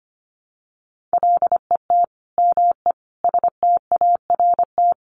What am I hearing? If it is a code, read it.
Morse code, heard as a keyed tone, about 25 wpm, 700 Hz. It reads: LET ME START
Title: Sets of 5 Words Encore 25wpm